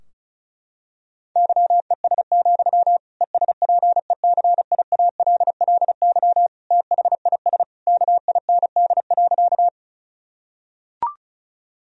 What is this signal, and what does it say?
Morse code, heard as a keyed tone, about 36 wpm, 700 Hz. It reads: YES, ESPECIALLY THIS KIND.
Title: Sentences from Top 1000 Words - Review Part 4 - 35wpm